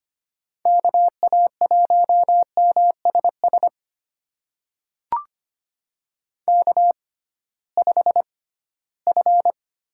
Morse code, heard as a keyed tone, about 25 wpm, spaced as for 16 wpm, 700 Hz. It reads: KA1MSS K5F
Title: USA Call Signs - 6x Character Spacing - 25wpm